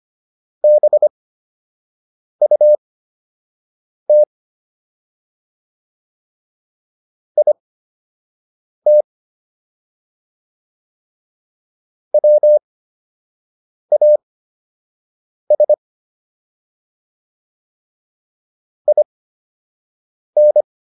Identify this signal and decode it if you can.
Morse code, heard as a keyed tone, about 25 wpm, spaced as for 6 wpm, 600 Hz. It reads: BUT IT WAS IN